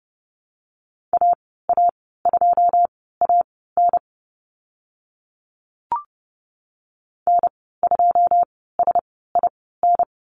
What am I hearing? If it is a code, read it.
Morse code, heard as a keyed tone, about 30 wpm, spaced as for 17 wpm, 700 Hz. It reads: AA2AN N2SIN
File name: Lesson 14g - Number 2 - Call Signs 30wpm